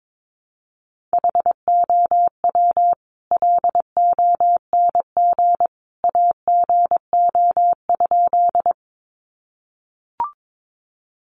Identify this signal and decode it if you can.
Morse code, heard as a keyed tone, about 22 wpm, 700 Hz. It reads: HOW LONG AGO?
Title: Sentences from Top 500 Words 22wpm